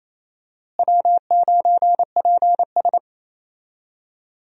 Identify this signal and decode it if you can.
Morse code, heard as a keyed tone, about 28 wpm, 700 Hz. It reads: W9PS
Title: USA Call Signs - 4 Characters - Repeated 3x - 28wpm